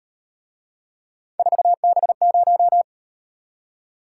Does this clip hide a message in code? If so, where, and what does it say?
Yes: Morse code from 1.4 to 2.8 s, heard as a keyed tone, about 38 wpm, 700 Hz. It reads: 4B0